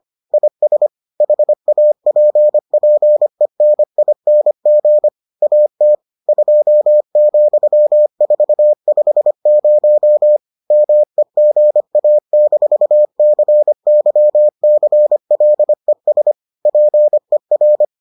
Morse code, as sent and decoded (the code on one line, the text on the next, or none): .. ... / .... .- .--. .--. -. .. -. --. / .- - / ..--- --..-- ....- ..... ----- / -- . --. .- -....- -.-. -.-- -.-. .-.. ... / .--. .-.
IS HAPPENING AT 2,450 MEGA-CYCLES PER